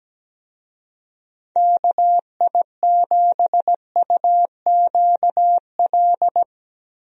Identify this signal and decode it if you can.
Morse code, heard as a keyed tone, about 17 wpm, 700 Hz. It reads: KI7UQL
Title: USA Call Signs - 6 Characters - Repeated 2x - 17wpm